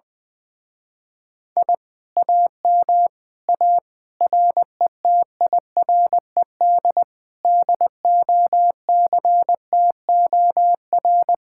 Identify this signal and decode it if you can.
Morse code, heard as a keyed tone, about 21 wpm, 700 Hz. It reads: I AM A RETIRED DOCTOR
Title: Speed-Racing Common QSO Phrases Encore - 20wpm